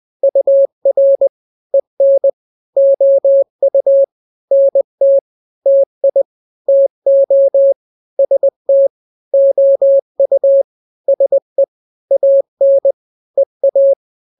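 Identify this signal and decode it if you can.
Morse code, heard as a keyed tone, about 20 wpm, 550 Hz. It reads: UR EN OU NT TI TO ST OU SE AN EA
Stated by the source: Morse drill of two-letter n-grams at 20/18 WPM (Farnsworth), from KK7RR.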